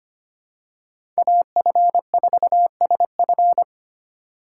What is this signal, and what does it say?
Morse code, heard as a keyed tone, about 25 wpm, 700 Hz. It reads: AF4SF
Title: USA Call Signs - 5 Characters - 2x Character Spacing - 25wpm